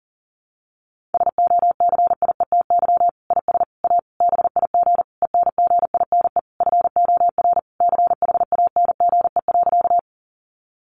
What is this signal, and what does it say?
Morse code, heard as a keyed tone, about 40 wpm, 700 Hz. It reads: SOCIETY IS A BIG ENGINE FOR CHANGE.
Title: Sentences from Top 1500 Words 40wpm